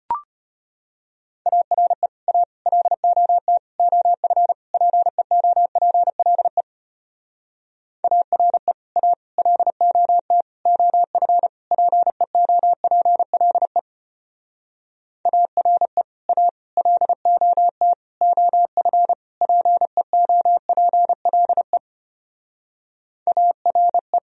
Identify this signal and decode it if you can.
Morse code, heard as a keyed tone, about 33 wpm, 700 Hz. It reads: ARE A LOT OF PEOPLE ARE A LOT OF PEOPLE ARE A LOT OF PEOPLE ARE